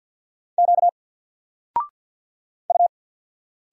Morse code, heard as a keyed tone, about 50 wpm, 700 Hz. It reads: Q U
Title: Single Letter-Number 50wpm